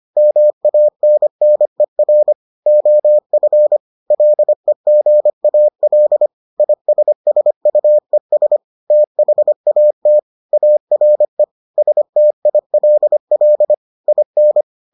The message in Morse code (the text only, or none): MANNER OF LEGAL ISSUES THAT ARE STILL IN